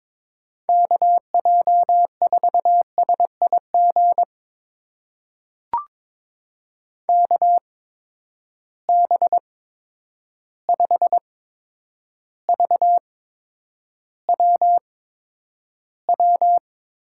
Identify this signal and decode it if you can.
Morse code, heard as a keyed tone, about 22 wpm, spaced as for 9 wpm, 700 Hz. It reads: KJ4SIG KB5VWW
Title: USA Call Signs - 6 Characters - 8x Character Spacing - 22wpm